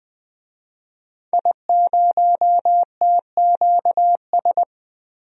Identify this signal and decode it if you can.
Morse code, heard as a keyed tone, about 20 wpm, 700 Hz. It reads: I0TQS
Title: DXCC-100 Call Signs - 5 Characters - 20wpm